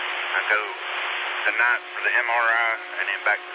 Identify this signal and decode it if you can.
Emergency Alert System (SAME) tones, heard as harsh buzzing bursts, with speech. It is heard from 0.2 to 3.0 s.